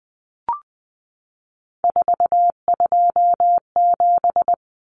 Morse code, heard as a keyed tone, about 20 wpm, 700 Hz. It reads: 427
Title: Numbers - 3-Digit 20wpm